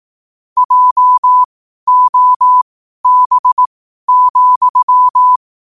Morse code, heard as a keyed tone, about 18 wpm, spaced as for 12 wpm, 1000 Hz. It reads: JOB,
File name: qotdfiles / 12wpm